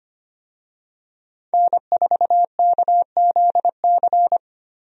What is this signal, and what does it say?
Morse code, heard as a keyed tone, about 25 wpm, 700 Hz. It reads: N4KZC
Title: USA Call Signs - 5 Characters - Repeated 3x - 25wpm